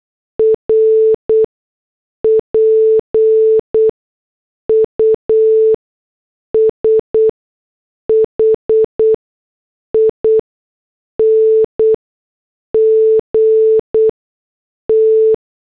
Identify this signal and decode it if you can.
Morse code, heard as a keyed tone, about 8 wpm, spaced as for 6 wpm, 440 Hz. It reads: R PUSHING T
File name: morse / headlines